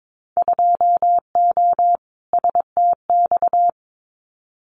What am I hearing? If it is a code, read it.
Morse code, heard as a keyed tone, about 22 wpm, 700 Hz. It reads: 2O STX
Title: Winter Field Day Exchanges 22wpm